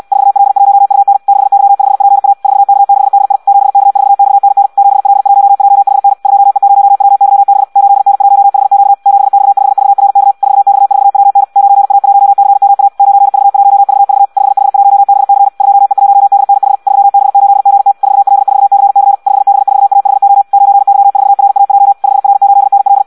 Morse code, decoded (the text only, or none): 4L1UN 6Y5PA 4K6DI 7K4XNN 7L1JHN 2E0WYB 8N1HQ 7X5VRK 4X4WN 9A1CMA 9V1VV 5H1BC 2E1DNH 3B8GT 4L4CC 5B4AHO 9Y4RAJ 5R8NL